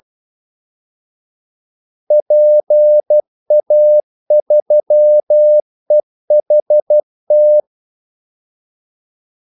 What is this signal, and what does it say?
Morse code, heard as a keyed tone, about 12 wpm, 600 Hz. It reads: PA3EHT
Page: 2 ekstra ordmellemrum mellem kaldesignalerne